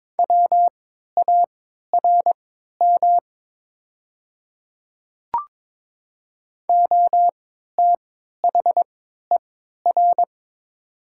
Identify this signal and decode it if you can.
Morse code, heard as a keyed tone, about 22 wpm, spaced as for 13 wpm, 700 Hz. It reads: WARM OTHER
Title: 3x Character Spacing - Top 500 Words - 22wpm